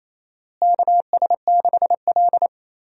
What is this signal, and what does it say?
Morse code, heard as a keyed tone, about 28 wpm, 700 Hz. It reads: KS6L
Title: USA Call Signs - 4 Characters - Repeated 2x - 28wpm